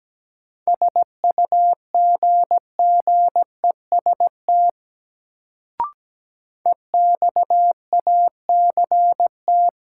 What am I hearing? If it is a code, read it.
Morse code, heard as a keyed tone, about 17 wpm, 700 Hz. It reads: SUGGEST EXACT
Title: Sentences from Top 1300 Words - New Word Drill Alt - 17wpm